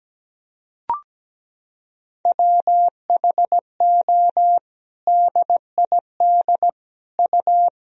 Morse code, heard as a keyed tone, about 17 wpm, 700 Hz. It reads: WHO DID U
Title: Sentences from Top 300 Words and Abbreviations 17wpm